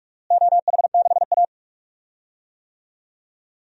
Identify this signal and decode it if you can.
Morse code, heard as a keyed tone, about 46 wpm, 700 Hz. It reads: OH6A
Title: DXCC-100 Call Signs - 4 Characters - Repeated 2x - 45wpm